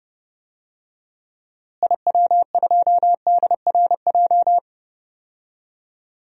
Morse code, heard as a keyed tone, about 30 wpm, 700 Hz. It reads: IW2DRJ
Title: DXCC-100 Call Signs - 6 Characters - Repeated 3x - 30wpm